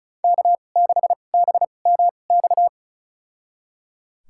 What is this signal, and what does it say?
Morse code, heard as a keyed tone, about 36 wpm, spaced as for 25 wpm, 700 Hz. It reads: K6BMX
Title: USA Call Signs - 5 Characters - 2x Character Spacing - 35wpm